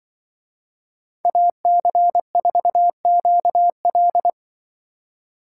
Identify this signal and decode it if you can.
Morse code, heard as a keyed tone, about 24 wpm, 700 Hz. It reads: AC4QL
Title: Speed Racing - USA 2x2 Call Signs 20wpm